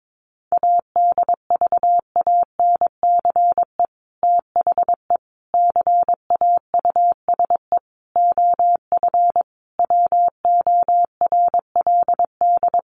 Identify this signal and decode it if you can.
Morse code, heard as a keyed tone, about 22 wpm, 700 Hz. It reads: ADVANCE THE CAUSE OF WORLD
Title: Sentences from Top 1400 Words 22wpm